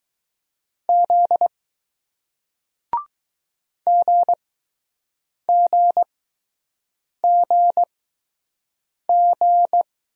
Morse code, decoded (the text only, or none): Z GGGG